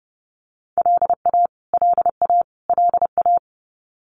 Morse code, heard as a keyed tone, about 30 wpm, 700 Hz. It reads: LA LA LA
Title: US States Repeated 3x 30wpm